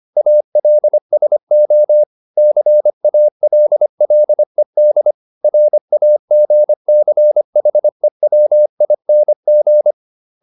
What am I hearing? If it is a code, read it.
Morse code, heard as a keyed tone, about 25 wpm, 600 Hz. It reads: ALSO CALLED RAGCHEWING